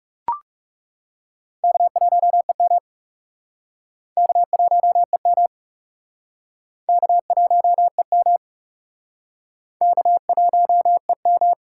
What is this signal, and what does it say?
Morse code, heard as a keyed tone, about 36 wpm, 700 Hz. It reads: K1EM K1EM K1EM K1EM
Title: Speed Racing - USA 1x2 Call Signs 30wpm